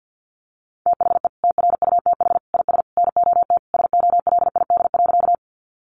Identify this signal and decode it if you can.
Morse code, heard as a keyed tone, about 50 wpm, 700 Hz. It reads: THE TRUTH IS NOT SOLID.
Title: Sentences from Top 1500 Words 50wpm